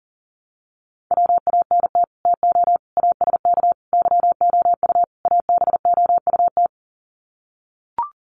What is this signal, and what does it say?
Morse code, heard as a keyed tone, about 40 wpm, 700 Hz. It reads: WANT TO ASK YOU ABOUT
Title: Sets of 5 Words Encore2 40wpm